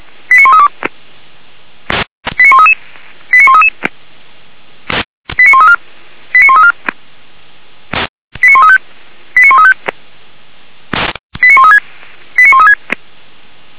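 Five-tone selective call (ZVEI), heard as quick runs of five tones, tones of about 70 ms. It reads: 89132 89133 89133 89134 89134 89135 89135 89136 89136
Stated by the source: ZVEI five-tone selective call received on an SDR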